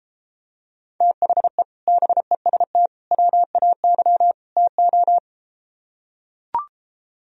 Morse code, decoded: THE BEST WAY TO

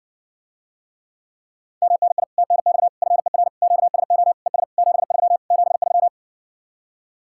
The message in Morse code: MNI TNX FER QSO ES 73 73

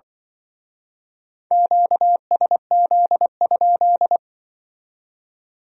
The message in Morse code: QSZ?

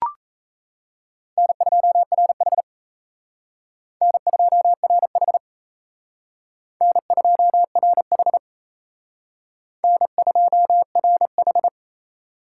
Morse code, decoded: N2RH N2RH N2RH N2RH

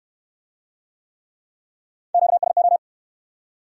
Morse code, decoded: QSO

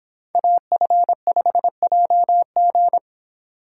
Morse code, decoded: AF5JG

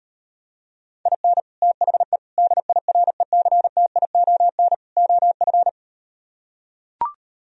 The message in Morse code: IN THE DIRECTION OF

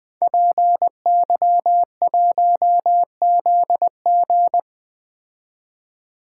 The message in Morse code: PY1ZG

